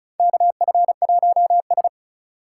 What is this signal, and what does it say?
Morse code, heard as a keyed tone, about 36 wpm, 700 Hz. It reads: KF1S
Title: USA Call Signs - 4 Characters - 2x Character Spacing - 35wpm